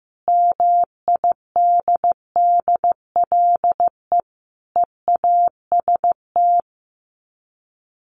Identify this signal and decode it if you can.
Morse code, heard as a keyed tone, about 15 wpm, 700 Hz. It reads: MIDDLE EAST
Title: Sets of 3 Words 15wpm